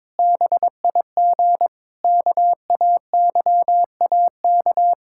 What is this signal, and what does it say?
Morse code, heard as a keyed tone, about 22 wpm, 700 Hz. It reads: BIG KAYAK
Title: Pangram Challenge v2 - 22wpm